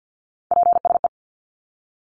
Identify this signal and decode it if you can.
Morse code, heard as a keyed tone, about 50 wpm, 700 Hz. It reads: PSE
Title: Abbreviations 50wpm